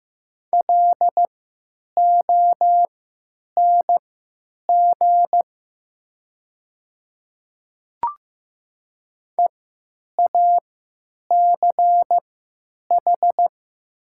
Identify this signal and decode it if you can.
Morse code, heard as a keyed tone, about 15 wpm, spaced as for 9 wpm, 700 Hz. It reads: LONG EACH
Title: 3x Character Spacing - Top 100 Words - 15wpm